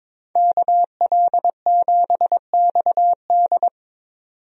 Morse code, heard as a keyed tone, about 22 wpm, 700 Hz. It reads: KL7XD
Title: DXCC-100 Call Signs - 5 Characters - 22wpm